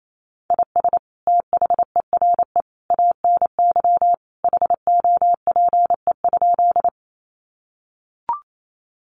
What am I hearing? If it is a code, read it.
Morse code, heard as a keyed tone, about 28 wpm, 700 Hz. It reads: IS THERE ANY HOPE?
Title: Sentences from Top 1000 Words 28wpm